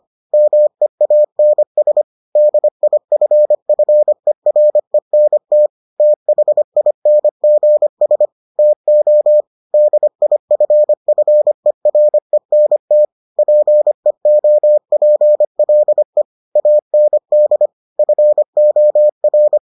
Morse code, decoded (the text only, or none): MEANS DIFFERENT THINGS TO DIFFERENT PEOPLE AND FOR